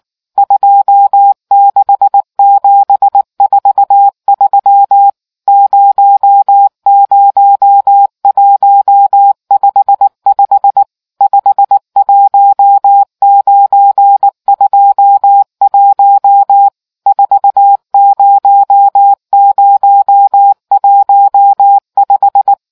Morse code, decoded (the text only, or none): 26743 00155 51921 40015